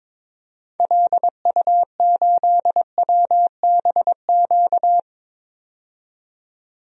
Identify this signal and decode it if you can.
Morse code, heard as a keyed tone, about 22 wpm, 700 Hz. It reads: LU8WBQ
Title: DXCC-100 Call Signs - 6 Characters - 22wpm